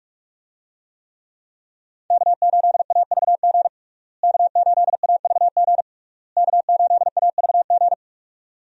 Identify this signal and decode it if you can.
Morse code, heard as a keyed tone, about 46 wpm, 700 Hz. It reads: K8AVG K8AVG K8AVG
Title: USA Call Signs - 5 Characters - Repeated 3x - 45wpm